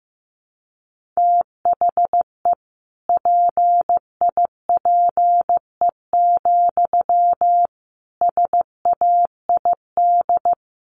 Morse code, heard as a keyed tone, about 15 wpm, 700 Hz. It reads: THE PIPE, SAID
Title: Sentences from Top 1500 Words 15wpm